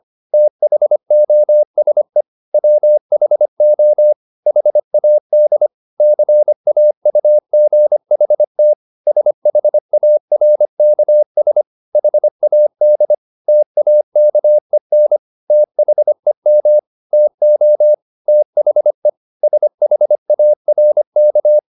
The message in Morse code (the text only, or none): THOSE WHO HAD CAUGHT SHARKS HAD TAKEN THEM TO THE SHARK